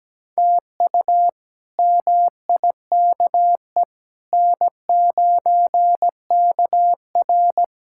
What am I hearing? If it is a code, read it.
Morse code, heard as a keyed tone, about 17 wpm, 700 Hz. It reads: TU MIKE N9KR
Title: K1USN SST QSO Practice - 17wpm